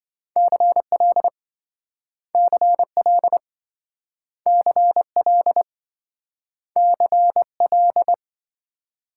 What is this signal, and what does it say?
Morse code, heard as a keyed tone, about 26 wpm, 700 Hz. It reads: CL CL CL CL